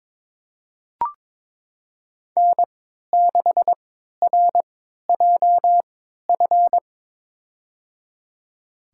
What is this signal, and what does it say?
Morse code, heard as a keyed tone, about 21 wpm, spaced as for 12 wpm, 700 Hz. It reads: N6RJF
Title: USA Call Signs - 5 Characters - 3x Character Spacing - 22wpm